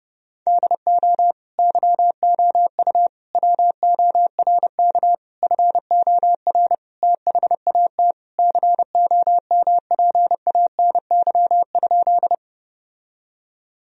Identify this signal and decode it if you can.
Morse code, heard as a keyed tone, about 30 wpm, 700 Hz. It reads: DO YOU WORK FOR THAT COMPANY?
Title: Sentences from Top 1000 Words - Review Part 3 - 30wpm